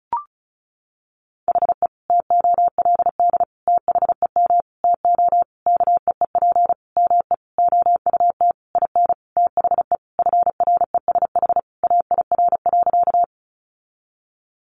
Morse code, heard as a keyed tone, about 34 wpm, 700 Hz. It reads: HE TOLD THEM TO KEEP ME OUT IN THE FRESH AIR.
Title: Sentences from Top 1200 Words 35wpm